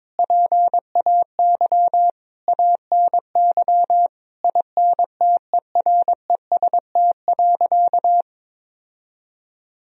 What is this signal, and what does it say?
Morse code, heard as a keyed tone, about 22 wpm, 700 Hz. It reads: PAY ANY INTEREST.
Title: Sentences from Top 1000 Words - Review Part 2 - 22wpm